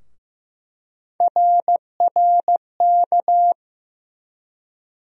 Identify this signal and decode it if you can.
Morse code, heard as a keyed tone, about 15 wpm, 700 Hz. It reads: RRK
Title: USA Call Sign Postfixes - 3 Letters 15wpm